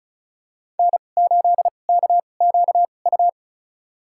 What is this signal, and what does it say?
Morse code, heard as a keyed tone, about 36 wpm, spaced as for 25 wpm, 700 Hz. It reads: N8KQU
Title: USA Call Signs - 5 Characters - 2x Character Spacing - 35wpm